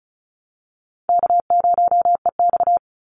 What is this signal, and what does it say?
Morse code, heard as a keyed tone, about 34 wpm, 700 Hz. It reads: K0EX
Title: Call Signs - 35wpm